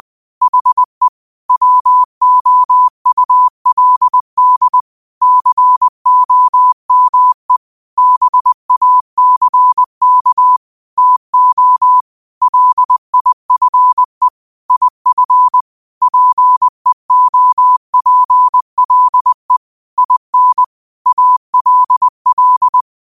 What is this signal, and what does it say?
Morse code, heard as a keyed tone, about 20 wpm, 1000 Hz. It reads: HE WOULD COME BACK TO LIFE IF PEOPLE IN ALL
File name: qotdfiles / 20wpm